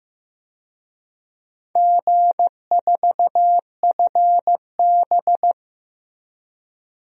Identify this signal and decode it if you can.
Morse code, heard as a keyed tone, about 15 wpm, 700 Hz. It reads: G4FB
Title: DXCC-100 Call Signs - 4 Characters - Repeated 2x - 15wpm